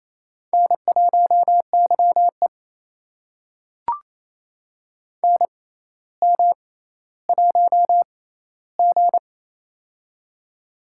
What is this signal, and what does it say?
Morse code, heard as a keyed tone, about 27 wpm, spaced as for 14 wpm, 700 Hz. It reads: N1YE NM1G
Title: USA Call Signs - 4 Characters - 6x Character Spacing - 28wpm